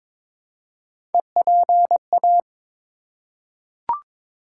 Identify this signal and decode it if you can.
Morse code, heard as a keyed tone, about 23 wpm, 700 Hz. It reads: EPA
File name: ARRL Sections v2 - 22wpm